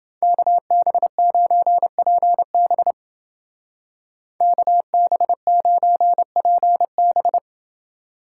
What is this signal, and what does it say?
Morse code, heard as a keyed tone, about 29 wpm, 700 Hz. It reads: KB9PB KB9PB